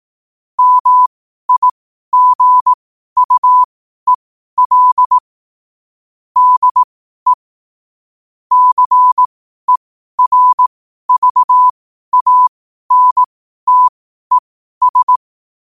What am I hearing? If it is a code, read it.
Morse code, heard as a keyed tone, about 18 wpm, spaced as for 12 wpm, 1000 Hz. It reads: MIGUEL DE CERVANTES
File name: qotdfiles / 12wpm